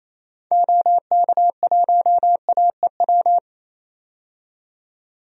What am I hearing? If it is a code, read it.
Morse code, heard as a keyed tone, about 28 wpm, 700 Hz. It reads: OK1AEW
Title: DXCC-100 Call Signs - 6 Characters - 28wpm